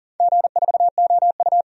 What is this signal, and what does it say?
Morse code, heard as a keyed tone, about 40 wpm, 700 Hz. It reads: G4OU
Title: DXCC-100 Call Signs - 4 Characters - 40wpm